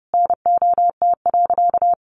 Morse code, heard as a keyed tone, about 30 wpm, 700 Hz. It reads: NOT.
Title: Sentences from Top 900 Words 30wpm